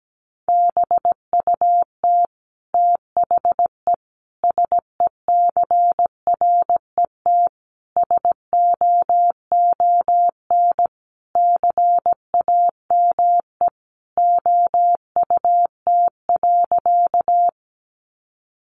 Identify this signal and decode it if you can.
Morse code, heard as a keyed tone, about 17 wpm, 700 Hz. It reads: BUT THE SECRET SOON CAME OUT.
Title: Sentences from Top 1200 Words 17wpm